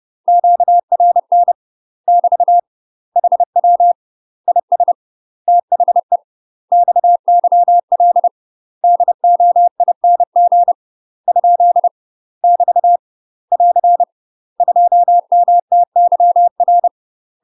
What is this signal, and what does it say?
Morse code, heard as a keyed tone, about 30 wpm, 700 Hz. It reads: QRN <BT> HW IS THE XYL DOING ? <BT> <AR> 2MTYR
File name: UK MASTER QSOs TRAINING/30WPM